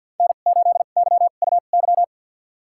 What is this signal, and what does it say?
Morse code, heard as a keyed tone, about 51 wpm, spaced as for 36 wpm, 700 Hz. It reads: N8YUY
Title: USA Call Signs - 5 Characters - 2x Character Spacing - 50wpm